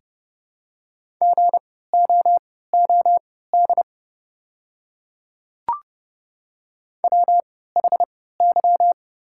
Morse code, heard as a keyed tone, about 29 wpm, spaced as for 17 wpm, 690 Hz. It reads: GOOD WHY
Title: 3x Character Spacing - Top 200 Words - 30wpm